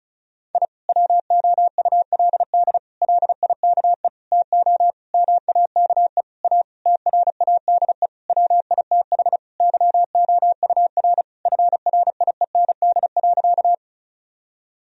Morse code, heard as a keyed tone, about 34 wpm, 700 Hz. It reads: I WOULD LIKE TO MAKE A TRADE WITH YOUR FRIEND.
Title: Sentences from Top 1000 Words - Review Part 4 - 35wpm